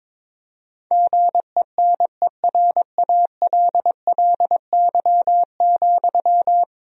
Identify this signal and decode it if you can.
Morse code, heard as a keyed tone, about 22 wpm, 700 Hz. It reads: GENERALLY,
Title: Sentences from Top 1000 Words - Review Part 4 - 22wpm